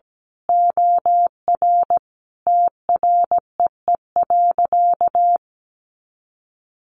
Morse code, heard as a keyed tone, about 17 wpm, 700 Hz. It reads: OR TREE.
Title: A Sound Of Thunder 17wpm